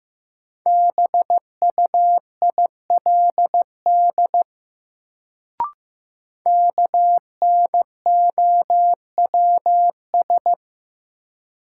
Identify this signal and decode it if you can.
Morse code, heard as a keyed tone, about 15 wpm, 700 Hz. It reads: BUILD KNOWS
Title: Sentences from Top 1000 Words - Review Part 1 - New Word Drill - 15wpm